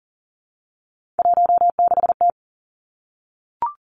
Morse code, heard as a keyed tone, about 38 wpm, 700 Hz. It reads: 16T
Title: Three Challenging Letter-Number Combinations 40wpm